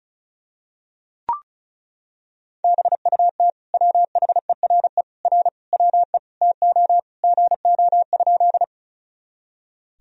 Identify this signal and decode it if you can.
Morse code, heard as a keyed tone, about 36 wpm, 700 Hz. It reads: BUT WHERE R WE TO GO?
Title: Sentences from Top 200 Words and Abbreviations 35wpm